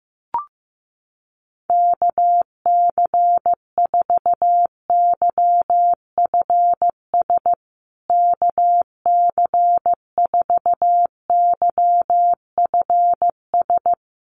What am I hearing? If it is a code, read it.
Morse code, heard as a keyed tone, about 15 wpm, 700 Hz. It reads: KC4YFS KC4YFS
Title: USA Call Signs Repeated 3x 15wpm